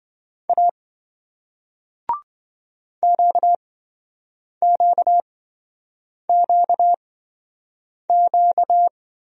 Morse code, heard as a keyed tone, about 27 wpm, spaced as for 7 wpm, 700 Hz. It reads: A QQQQ